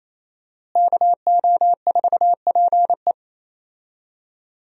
Morse code, heard as a keyed tone, about 28 wpm, 700 Hz. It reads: KO4PE
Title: USA Call Signs - 5 Characters - Repeated 2x - 28wpm